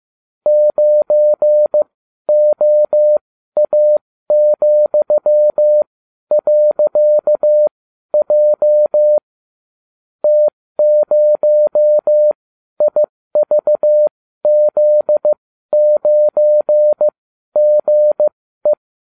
Morse code, heard as a keyed tone, about 15 wpm, spaced as for 12 wpm, 600 Hz. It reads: 9OA,.J T0IVZ9GE